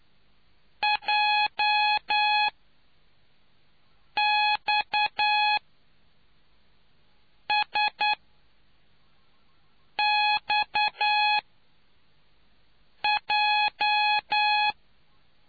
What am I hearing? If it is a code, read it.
Morse code, heard as a keyed tone, about 9 wpm, spaced as for 4 wpm, 840 Hz. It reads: JXSXJ